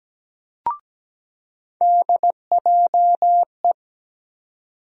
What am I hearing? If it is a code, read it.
Morse code, heard as a keyed tone, about 17 wpm, 700 Hz. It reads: DJE
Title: USA Call Sign Postfixes - 3 Letters 17wpm